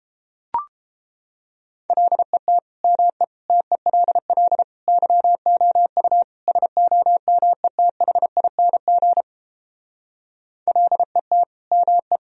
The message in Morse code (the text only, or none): LET ME TELL YOU SOMETHING LET ME